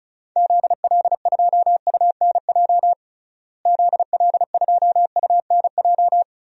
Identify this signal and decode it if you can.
Morse code, heard as a keyed tone, about 36 wpm, 700 Hz. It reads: ZL2UNJ ZL2UNJ
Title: DXCC-100 Call Signs - 6 Characters - Repeated 3x - 35wpm